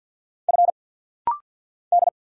Morse code, heard as a keyed tone, about 48 wpm, 690 Hz. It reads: F D
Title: Single Letter-Number - Rapid-Fire 50wpm